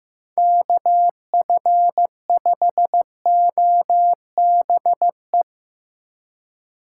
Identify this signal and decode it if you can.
Morse code, heard as a keyed tone, about 15 wpm, 700 Hz. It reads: KF5OBE
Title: USA Call Signs - 6 Characters - 6x Character Spacing - 15wpm